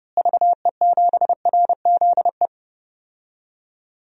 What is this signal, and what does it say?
Morse code, heard as a keyed tone, about 32 wpm, 700 Hz. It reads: VE7RZE
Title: DXCC-100 Call Signs - 6 Characters - Repeated 3x - 30wpm